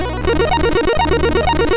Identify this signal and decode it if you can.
There is no code.